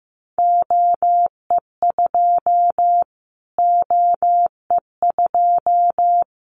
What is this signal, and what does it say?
Morse code, heard as a keyed tone, about 15 wpm, 700 Hz. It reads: OE2 OE2
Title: Callsign Prefixes for Top 25 DXCC - Repeated 3x - 15wpm